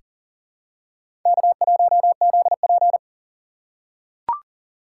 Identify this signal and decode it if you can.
Morse code, heard as a keyed tone, about 40 wpm, 700 Hz. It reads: K1ZP
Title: USA Call Signs - 4 Characters - 4x Character Spacing - 40wpm